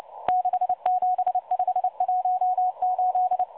Morse code, heard as a keyed tone, about 29 wpm, 730 Hz. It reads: 67518
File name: M12 - 8116 kHz CW 2-11-14